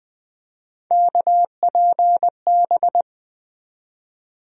Morse code, heard as a keyed tone, about 20 wpm, 700 Hz. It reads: KPB
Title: USA Call Sign Postfixes - 3 Letters 20wpm